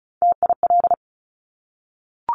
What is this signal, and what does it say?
Morse code, heard as a keyed tone, about 36 wpm, 700 Hz. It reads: TIL
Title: Top Three Letter Combinations in Words 35wpm